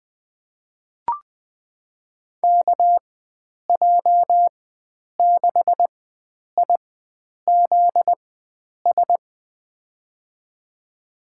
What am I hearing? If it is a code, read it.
Morse code, heard as a keyed tone, about 20 wpm, spaced as for 9 wpm, 700 Hz. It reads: KJ6IZS
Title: USA Call Signs - 4x Character Spacing - 20wpm